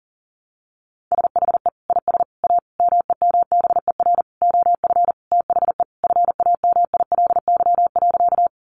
Morse code, heard as a keyed tone, about 40 wpm, 700 Hz. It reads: SHE IS A MEMBER OF THE FAMILY.
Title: Sentences from Top 1100 Words 40wpm